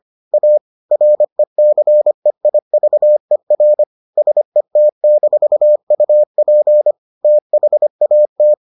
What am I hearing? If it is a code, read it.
Morse code, heard as a keyed tone, about 25 wpm, 600 Hz. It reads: A RECEIVER SET-UP THAT